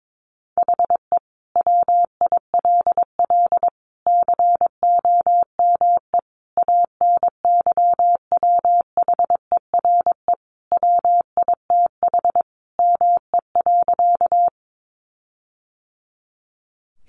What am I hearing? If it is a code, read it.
Morse code, heard as a keyed tone, about 22 wpm, 700 Hz. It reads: HE WILL COME ANYWHERE WITH ME.
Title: Sentences from Top 1400 Words 22wpm